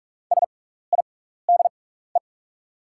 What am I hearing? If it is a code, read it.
Morse code, heard as a keyed tone, about 46 wpm, spaced as for 16 wpm, 690 Hz. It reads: SIDE
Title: 6x Character Spacing - Top 300 Words - 45wpm